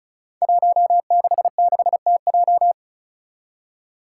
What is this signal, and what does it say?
Morse code, heard as a keyed tone, about 36 wpm, 700 Hz. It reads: 166TJ
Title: Five Challenging Letter-Number Combinations 35wpm